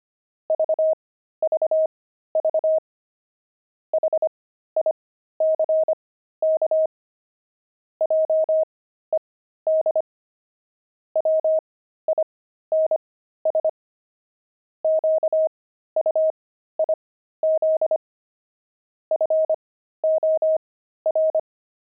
Morse code, heard as a keyed tone, about 25 wpm, spaced as for 13 wpm, 630 Hz. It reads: VVV HICK JED WINS QUIZ FOR
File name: pangram-fs-13-cs-25-blimp